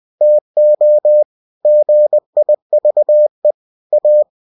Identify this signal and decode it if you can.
Morse code, heard as a keyed tone, about 20 wpm, 600 Hz. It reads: TO GIVE A
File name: cw-books / winnie-the-pooh / winnie-20wpm-page-0102